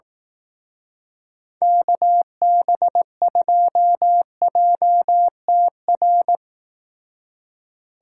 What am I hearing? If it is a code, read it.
Morse code, heard as a keyed tone, about 18 wpm, 700 Hz. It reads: KB2JTR
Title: Speed Racing - USA Call Signs 15wpm